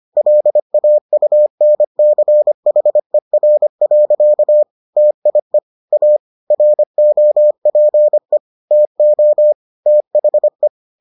Morse code, heard as a keyed tone, about 25 wpm, 600 Hz. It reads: LAUNCHER. TIE A ROPE TO THE